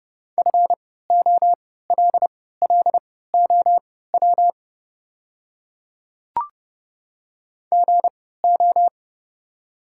Morse code, heard as a keyed tone, about 30 wpm, spaced as for 17 wpm, 700 Hz. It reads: FOLLOW GO
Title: 3x Character Spacing - Top 200 Words - 30wpm